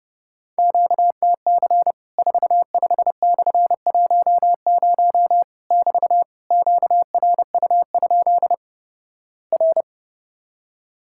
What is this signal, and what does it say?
Morse code, heard as a keyed tone, about 31 wpm, 700 Hz. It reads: QTC 45/10 <BT> QRU? R
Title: Worked All Europe DX Contest - QTC Exchange - 30wpm